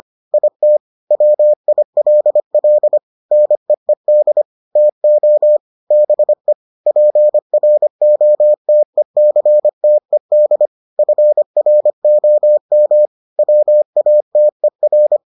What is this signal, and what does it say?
Morse code, heard as a keyed tone, about 25 wpm, 600 Hz. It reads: IT WILL NEED TO BE PROTECTED FROM WATER